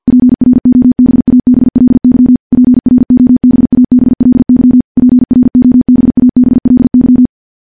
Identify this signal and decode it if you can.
Morse code, heard as a keyed tone, about 50 wpm, 250 Hz. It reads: PROBABLY PROBABLY PROBABLY
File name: morse / morse-training-50WPM